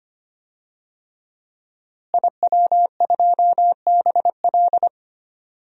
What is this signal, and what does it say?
Morse code, heard as a keyed tone, about 25 wpm, 700 Hz. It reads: IW2BL
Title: DXCC-100 Call Signs - 5 Characters - Repeated 3x - 25wpm